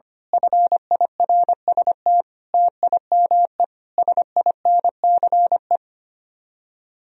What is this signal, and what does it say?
Morse code, heard as a keyed tone, about 24 wpm, 700 Hz. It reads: FIRST TIME SINCE
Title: Speed-Racing - Sets of 5 Words 25wpm